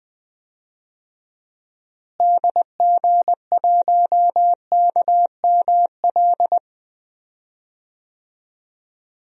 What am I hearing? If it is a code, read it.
Morse code, heard as a keyed tone, about 21 wpm, 700 Hz. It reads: DG1KML
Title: DXCC-100 Call Signs - 6 Characters - Repeated 3x - 20wpm